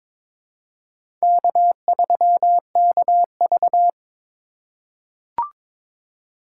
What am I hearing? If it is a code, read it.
Morse code, heard as a keyed tone, about 22 wpm, 700 Hz. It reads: K3KV
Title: USA Call Signs - 4 Characters - 3x Character Spacing - 22wpm